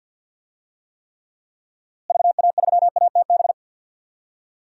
Morse code, heard as a keyed tone, about 46 wpm, 700 Hz. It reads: VA3ATB
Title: DXCC-100 Call Signs - 6 Characters - Repeated 3x - 50wpm